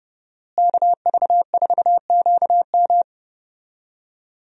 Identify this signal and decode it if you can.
Morse code, heard as a keyed tone, about 30 wpm, 700 Hz. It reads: KV4QM
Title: USA Call Signs - 5 Characters - Repeated 2x - 30wpm